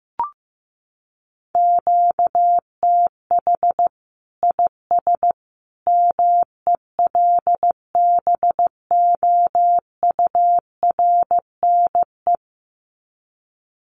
Morse code, heard as a keyed tone, about 15 wpm, 700 Hz. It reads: QTH IS MELBOURNE
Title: Common QSO Phrases 15wpm